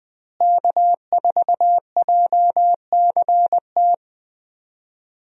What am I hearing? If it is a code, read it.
Morse code, heard as a keyed tone, about 20 wpm, 700 Hz. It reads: K4JCT
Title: USA Call Signs - 5 Characters - Repeated 3x - 20wpm